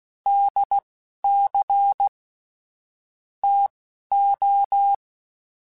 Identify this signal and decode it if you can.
Morse code, heard as a keyed tone, about 16 wpm, spaced as for 11 wpm, 800 Hz. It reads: DC TO
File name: Technical Subjects / Technical Subjects-mp3s / Sw Pwr Sup @10